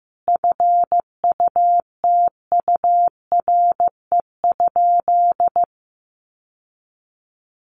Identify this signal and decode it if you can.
Morse code, heard as a keyed tone, about 15 wpm, 700 Hz. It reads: FUTURE?
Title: Sentences from Top 700 Words 15wpm